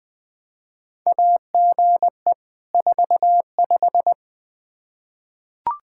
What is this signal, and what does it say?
Morse code, heard as a keyed tone, about 21 wpm, 700 Hz. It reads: AGE 45
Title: Practice QSO - Exchange 3 - More Info - 20wpm